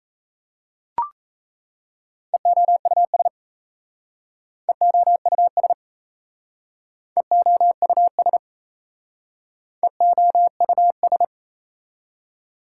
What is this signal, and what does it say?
Morse code, heard as a keyed tone, about 38 wpm, 690 Hz. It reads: EOUS EOUS EOUS EOUS